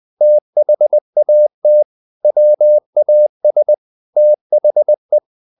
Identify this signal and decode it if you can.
Morse code, heard as a keyed tone, about 20 wpm, 600 Hz. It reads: THAT WAS THE
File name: the old man and the sea / the old man and the sea part 1 20wpm